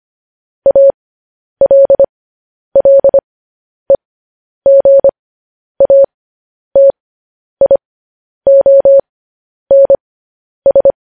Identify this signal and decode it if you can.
Morse code, heard as a keyed tone, about 25 wpm, spaced as for 10 wpm, 550 Hz. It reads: ALLEGATIONS
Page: news-headlines-in-morse-code-at-10-wpm